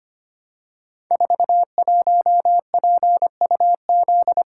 Morse code, heard as a keyed tone, about 25 wpm, 700 Hz. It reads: 41PUZ
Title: Code Groups - 5 Letter:Numbers - 25wpm